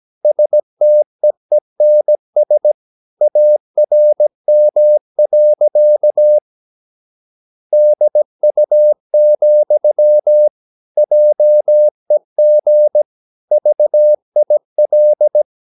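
Morse code, heard as a keyed tone, about 17 wpm, 600 Hz. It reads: STEENS ARM. DU, JEG VIL
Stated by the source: Slow (QRS) Morse practice at 17 wpm, from CW med Gnister.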